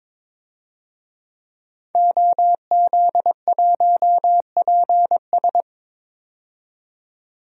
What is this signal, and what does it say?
Morse code, heard as a keyed tone, about 22 wpm, 700 Hz. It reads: OZ1PS
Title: DXCC-100 Call Signs - 5 Characters - 22wpm